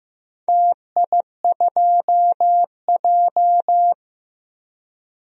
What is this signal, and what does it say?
Morse code, heard as a keyed tone, about 15 wpm, 700 Hz. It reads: TI2J
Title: DXCC-100 Call Signs - 4 Characters - 15wpm